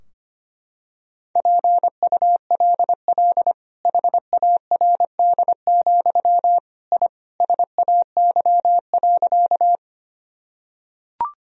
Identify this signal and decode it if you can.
Morse code, heard as a keyed tone, about 25 wpm, 700 Hz. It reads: PULL HARD, I SAY.
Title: Sentences from Top 1000 Words - Review Part 2 - 25wpm